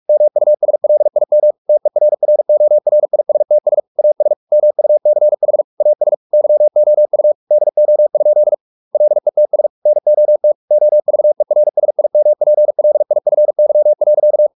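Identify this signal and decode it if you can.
Morse code, heard as a keyed tone, about 44 wpm, 600 Hz. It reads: MUSLIM TERRORISTS AS MUCH AS YOU DO? LETS NOT OVERSIMPLIFY.
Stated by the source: Morse practice at 45 wpm, the novel Never by Ken Follett in English, from CW med Gnister.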